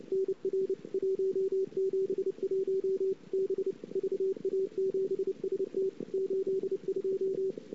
Morse code, heard as a keyed tone, about 32 wpm, 410 Hz. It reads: NR 171 B VA7ST 82
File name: CW / 2008-ss-cw-k5tr-op-n5rz -23:00